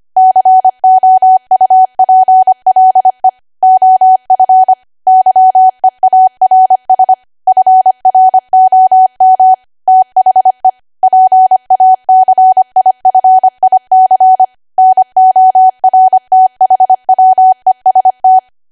Morse code, heard as a keyed tone, about 25 wpm, 750 Hz. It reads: COUPLE OF YEARS FROM THE PACIFIC NORTHWEST
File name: cwa / W1AW-Level3 / 25WPM